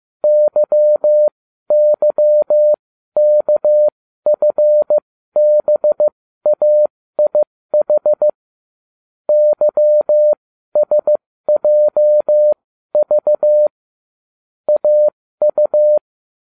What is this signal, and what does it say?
Morse code, heard as a keyed tone, about 15 wpm, spaced as for 12 wpm, 600 Hz. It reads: YYKFBAIH YSJV AU
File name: cw / ubacw30 3